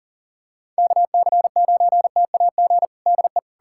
Morse code, heard as a keyed tone, about 40 wpm, 700 Hz. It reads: KC9TAG DE
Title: USA Callsign DE Callsign - 6 Characters - 40wpm